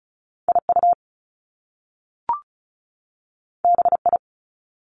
Morse code, heard as a keyed tone, about 36 wpm, 700 Hz. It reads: IU BI
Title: Top Two Letter Combinations in Words 35wpm